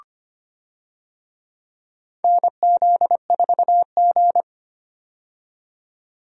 Morse code, heard as a keyed tone, about 25 wpm, 700 Hz. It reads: NZ4G
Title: USA Call Signs - 4 Characters - 25wpm